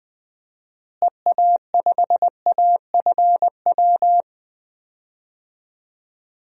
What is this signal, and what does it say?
Morse code, heard as a keyed tone, about 20 wpm, 700 Hz. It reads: EA5AFW
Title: DXCC-100 Call Signs - 6 Characters - Repeated 2x - 20wpm